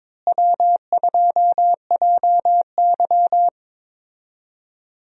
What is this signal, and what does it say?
Morse code, heard as a keyed tone, about 22 wpm, 700 Hz. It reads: W2JY
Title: USA Call Signs - 4 Characters - Repeated 3x - 22wpm